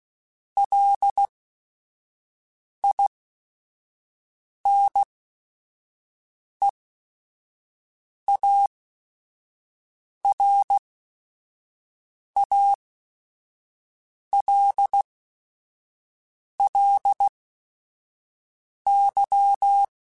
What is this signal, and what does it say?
Morse code, heard as a keyed tone, about 16 wpm, spaced as for 5 wpm, 800 Hz. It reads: LINEARALLY
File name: Technical Subjects / Technical Subjects-mp3s / XL @5